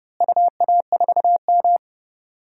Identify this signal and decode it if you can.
Morse code, heard as a keyed tone, about 32 wpm, 700 Hz. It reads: UA4M
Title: DXCC-100 Call Signs - 4 Characters - Repeated 2x - 30wpm